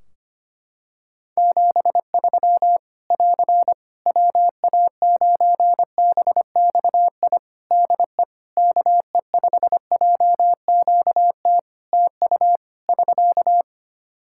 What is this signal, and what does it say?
Morse code, heard as a keyed tone, about 25 wpm, 700 Hz. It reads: 73 <AR> WA9BXI DE KE5JQT TU <SK>
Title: Practice QSO - Exchange 4 - Ending - 25wpm